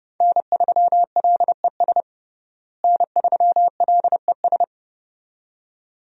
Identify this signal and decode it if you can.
Morse code, heard as a keyed tone, about 30 wpm, 700 Hz. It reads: N3LES N3LES
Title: USA Call Signs - 5 Characters - Repeated 2x - 30wpm